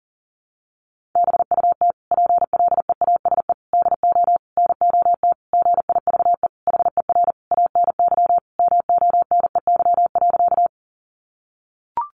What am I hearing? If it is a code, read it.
Morse code, heard as a keyed tone, about 40 wpm, 700 Hz. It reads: BUT PLEASE DO NOT GIVE HER ANY MONEY.
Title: Sentences from Top 1500 Words 40wpm